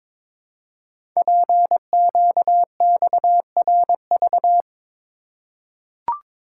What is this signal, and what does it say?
Morse code, heard as a keyed tone, about 22 wpm, 700 Hz. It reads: PQXRV
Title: Code Groups - 5 Letters - 22wpm